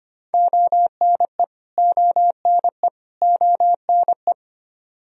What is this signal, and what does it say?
Morse code, heard as a keyed tone, about 25 wpm, 700 Hz. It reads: ONE ONE ONE
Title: Top 100 Words Repeated 25wpm